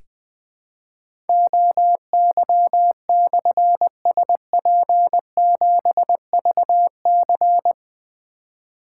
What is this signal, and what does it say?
Morse code, heard as a keyed tone, about 20 wpm, 700 Hz. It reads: OY/SP7VC
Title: Difficult Call Signs 20wpm